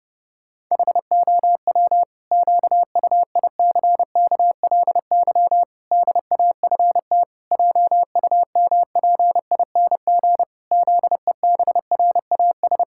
Morse code, heard as a keyed tone, about 30 wpm, 700 Hz. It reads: HOW QUICKLY DAFT JUMPING ZEBRAS
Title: Pangram Challenge v2 - 30wpm